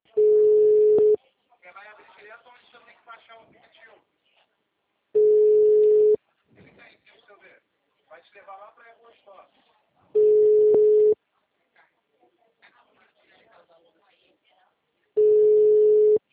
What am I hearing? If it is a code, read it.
Ringback tone (Europe, 425 Hz), with speech.